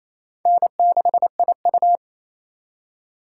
Morse code, heard as a keyed tone, about 37 wpm, 700 Hz. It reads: N6IU